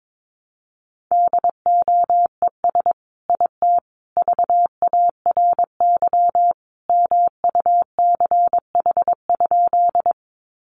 Morse code, heard as a keyed tone, about 22 wpm, 700 Hz. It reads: DOES IT VARY MUCH?
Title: Sentences from Top 1200 Words 22wpm